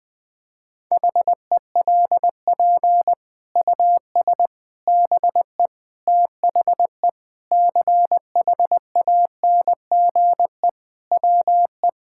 Morse code, heard as a keyed tone, about 20 wpm, 700 Hz. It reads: HELP US BE THE CHANGE WE
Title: Sets of 9 Words - 20wpm